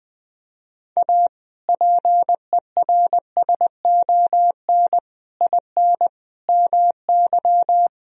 Morse code, heard as a keyed tone, about 21 wpm, 700 Hz. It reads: A PERSON IN MY
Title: Sentences from Top 1000 Words - Review Part 4 - 20wpm